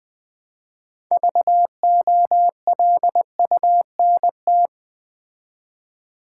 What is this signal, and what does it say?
Morse code, heard as a keyed tone, about 20 wpm, 700 Hz. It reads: VOLUNT
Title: Build A Word - 3 Letters At A Time - 9-Letter Words - 20wpm